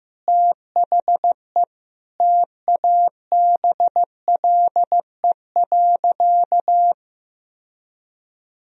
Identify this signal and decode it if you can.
Morse code, heard as a keyed tone, about 15 wpm, 700 Hz. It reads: THE TABLE.
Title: Sentences from Top 1000 Words - Review Part 2 - 15wpm